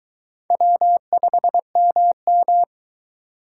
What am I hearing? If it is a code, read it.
Morse code, heard as a keyed tone, about 23 wpm, 700 Hz. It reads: W5MM